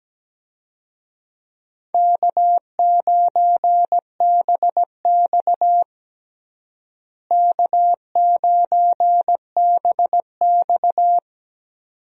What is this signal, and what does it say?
Morse code, heard as a keyed tone, about 17 wpm, 700 Hz. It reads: K9BX K9BX
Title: USA Call Signs - 4 Characters - Repeated 2x - 17wpm